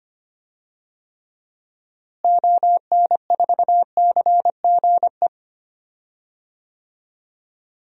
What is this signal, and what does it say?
Morse code, heard as a keyed tone, about 25 wpm, 700 Hz. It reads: ON4CGE
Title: DXCC-100 Call Signs - 6 Characters - Repeated 2x - 25wpm